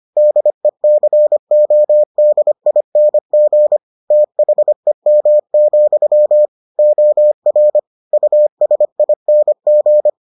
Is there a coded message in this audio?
Yes: Morse code from 0.2 to 10.1 s, heard as a keyed tone, about 25 wpm, 600 Hz. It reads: DECODING THEM, OR USING